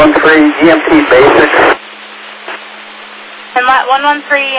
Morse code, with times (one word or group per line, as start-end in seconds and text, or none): none